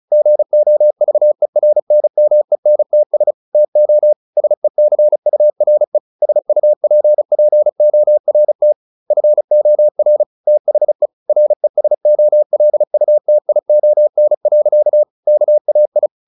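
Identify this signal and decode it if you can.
Morse code, heard as a keyed tone, about 35 wpm, 600 Hz. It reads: GOVERNMENTS TO SECURE SUPPORT FOR THE RESOLUTION. KAI
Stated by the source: Morse practice at 35 wpm from CW med Gnister, sending Ken Follett's novel Never in English.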